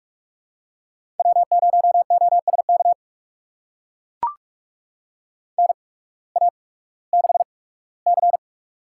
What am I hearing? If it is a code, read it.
Morse code, heard as a keyed tone, about 44 wpm, spaced as for 21 wpm, 700 Hz. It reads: W0OSK NA6C
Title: USA Call Signs - 8x Character Spacing - 45wpm